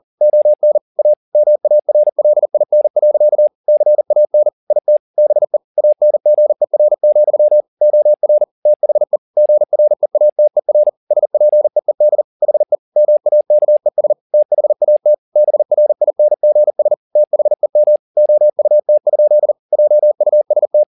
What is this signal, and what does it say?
Morse code, heard as a keyed tone, about 40 wpm, 600 Hz. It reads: ON A MARLIN. CAN IT BE ANGER, OR THE GREATER SPEED HE MAKES THAT BRINGS THEM OUT? JUST
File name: the old man and the sea 2 40wpm